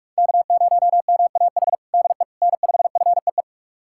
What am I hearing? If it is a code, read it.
Morse code, heard as a keyed tone, about 46 wpm, 690 Hz. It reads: K0MAH DE N5FEE
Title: USA Callsign DE Callsign 45wpm